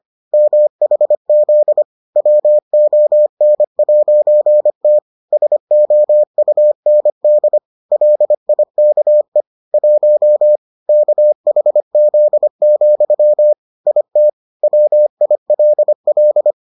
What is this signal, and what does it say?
Morse code, heard as a keyed tone, about 25 wpm, 600 Hz. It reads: MHZ WON'T SOUND LIKE 1 KHZ, IT WILL